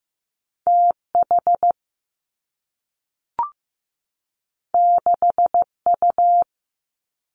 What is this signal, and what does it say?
Morse code, heard as a keyed tone, about 15 wpm, 700 Hz. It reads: T H 6 U
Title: Two Challenging Letter-Number Combinations 15wpm